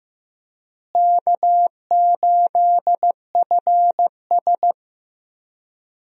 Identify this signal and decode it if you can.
Morse code, heard as a keyed tone, about 16 wpm, 700 Hz. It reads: K8FS
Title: USA Call Signs - 4 Characters - Repeated 3x - 15wpm